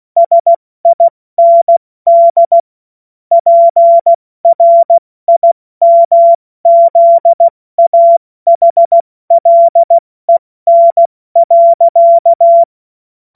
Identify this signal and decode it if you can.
Morse code, heard as a keyed tone, about 16 wpm, 680 Hz. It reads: SIND PRIMZAHLEN.
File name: Morsekurs / Witze / Witz 12. April 2022 70 F80